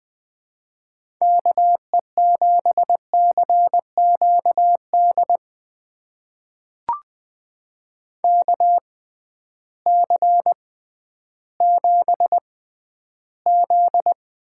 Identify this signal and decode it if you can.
Morse code, heard as a keyed tone, about 20 wpm, spaced as for 12 wpm, 700 Hz. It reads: KE7CQD KC7Z
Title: USA Call Signs - 6 Characters - 6x Character Spacing - 20wpm